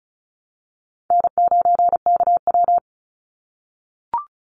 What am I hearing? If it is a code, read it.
Morse code, heard as a keyed tone, about 34 wpm, 700 Hz. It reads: N9KW
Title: NA SOTA Chaser Callsigns 35wpm